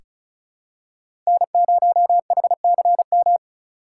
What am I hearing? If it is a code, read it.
Morse code, heard as a keyed tone, about 34 wpm, 700 Hz. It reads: N0HCM
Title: USA Call Signs - 5 Characters - 4x Character Spacing - 35wpm